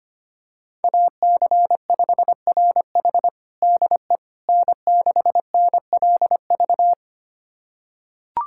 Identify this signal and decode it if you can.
Morse code, heard as a keyed tone, about 25 wpm, 700 Hz. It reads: AC5RH DE N6NLV
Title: USA Callsign DE Callsign 25wpm